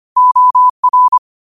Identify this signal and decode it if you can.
Morse code, heard as a keyed tone, about 25 wpm, 1000 Hz. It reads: OR
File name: qotdfiles / 25wpm